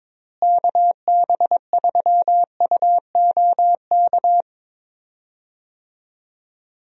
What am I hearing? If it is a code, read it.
Morse code, heard as a keyed tone, about 22 wpm, 700 Hz. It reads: KB3UOK
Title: USA Call Signs - 6 Characters - Repeated 3x - 22wpm